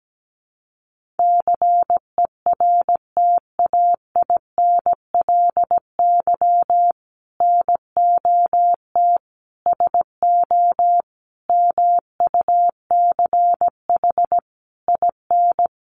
Morse code, heard as a keyed tone, about 17 wpm, 700 Hz. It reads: CERTAINLY NOT SO MUCH IN
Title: Sentences from Top 1500 Words 17wpm